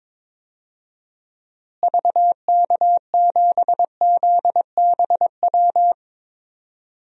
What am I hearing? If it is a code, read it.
Morse code, heard as a keyed tone, about 22 wpm, 700 Hz. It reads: VK7ZBW
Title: DXCC-100 Call Signs - 6 Characters - Repeated 2x - 22wpm